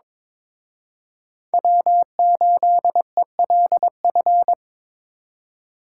Morse code, heard as a keyed tone, about 26 wpm, 700 Hz. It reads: W8ELF
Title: Speed Racing - USA 1x3 Call Signs 22wpm